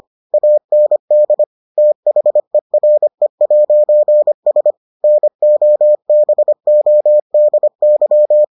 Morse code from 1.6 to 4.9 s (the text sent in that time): THERE'S